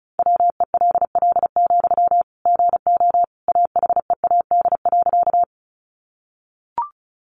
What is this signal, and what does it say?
Morse code, heard as a keyed tone, about 34 wpm, 700 Hz. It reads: WELL, GO AHEAD.
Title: Sentences from Top 1100 Words 35wpm